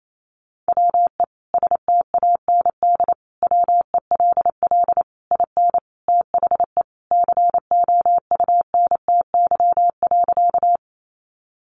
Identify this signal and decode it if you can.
Morse code, heard as a keyed tone, about 28 wpm, 700 Hz. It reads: WE STAND WELL IN THE COUNTY.
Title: Sentences from Top 1200 Words 28wpm